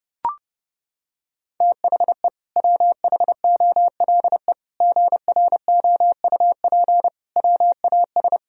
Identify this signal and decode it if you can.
Morse code, heard as a keyed tone, about 30 wpm, 700 Hz. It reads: THE WHOLE GROUP WAS
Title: Sentences from Top 1000 Words - Review Part 1 - 30wpm